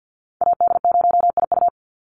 Morse code, heard as a keyed tone, about 50 wpm, 700 Hz. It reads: AD0IU
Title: Call Signs - 50wpm